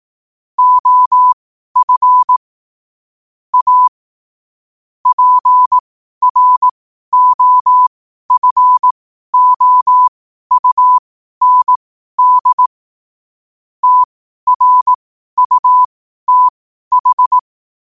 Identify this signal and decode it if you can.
Morse code, heard as a keyed tone, about 18 wpm, spaced as for 12 wpm, 1000 Hz. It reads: OF A PROFOUND TRUTH
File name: qotdfiles / 12wpm